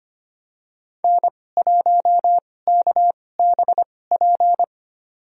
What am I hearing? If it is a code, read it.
Morse code, heard as a keyed tone, about 25 wpm, spaced as for 18 wpm, 700 Hz. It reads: N1KBP
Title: USA Call Signs - 2x Character Spacing - 25wpm